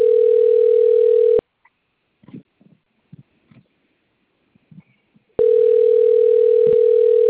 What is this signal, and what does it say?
Ringback tone (North America), 440 + 480 Hz.
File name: rg-1600-unknown- - - .20061